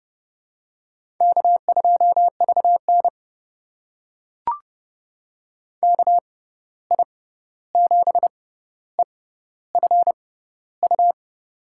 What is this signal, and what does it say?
Morse code, heard as a keyed tone, about 30 wpm, spaced as for 14 wpm, 700 Hz. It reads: K2VN KI7EFU
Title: USA Call Signs - 6x Character Spacing - 30wpm